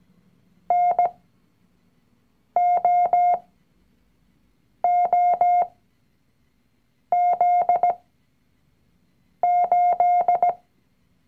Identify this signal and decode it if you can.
Morse code, heard as a keyed tone, about 17 wpm, spaced as for 5 wpm, 710 Hz. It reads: NOOZ8
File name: CW-Online-Kurs / Anfänger / Lektion 019